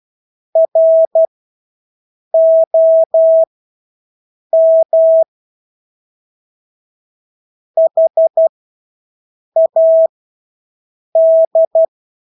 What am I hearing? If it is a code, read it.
Morse code, heard as a keyed tone, about 12 wpm, spaced as for 6 wpm, 650 Hz. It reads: ROM HAD
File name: Engelske ord 12-6 wpm